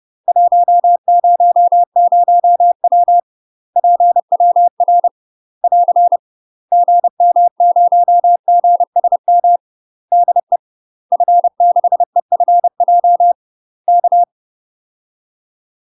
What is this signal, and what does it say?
Morse code, heard as a keyed tone, about 30 wpm, 700 Hz. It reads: 100W PWR <AR> GM0GSM DE F6EFJ K
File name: UK MASTER QSOs TRAINING / 30WPM / UK MASTER QSOs 0041